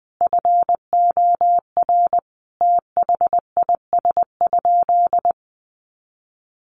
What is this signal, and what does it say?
Morse code, heard as a keyed tone, about 20 wpm, 700 Hz. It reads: FOR THIS?
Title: Sentences from Top 1400 Words 20wpm